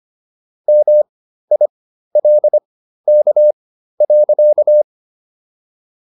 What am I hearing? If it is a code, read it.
Morse code, heard as a keyed tone, about 25 wpm, spaced as for 13 wpm, 600 Hz. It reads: MILK.